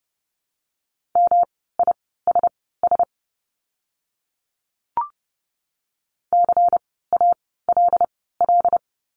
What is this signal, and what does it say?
Morse code, heard as a keyed tone, about 30 wpm, spaced as for 17 wpm, 700 Hz. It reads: MISS CALL
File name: course / 30:10wpm / 108 Lesson 18f - Letter m - Words 30wpm